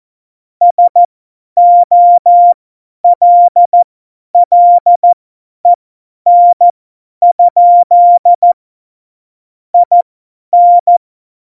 Morse code, heard as a keyed tone, about 14 wpm, spaced as for 10 wpm, 700 Hz. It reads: SOLLEN? IN
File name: Morsekurs / Witze / Witz 13. Oktober 2020, 50 F70